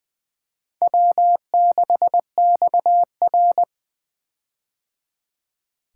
Morse code, heard as a keyed tone, about 19 wpm, 700 Hz. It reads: W6XR
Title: K1USN SST QSO Practice - 20wpm